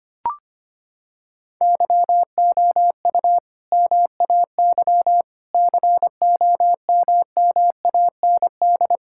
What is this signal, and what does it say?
Morse code, heard as a keyed tone, about 25 wpm, 700 Hz. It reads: YOU MAY COMMAND
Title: Sentences from Top 1600 Words 25wpm